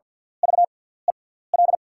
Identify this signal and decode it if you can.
Morse code, heard as a keyed tone, about 48 wpm, spaced as for 17 wpm, 700 Hz. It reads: VEL
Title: 6x Character Spacing - Top 500 Words - 50wpm